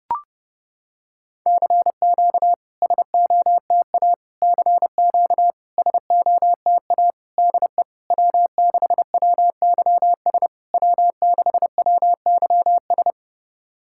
Morse code, heard as a keyed tone, about 30 wpm, 700 Hz. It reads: CQ SOTA CQ SOTA DE W6WYS W6WYS
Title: SOTA QSO Practice 30wpm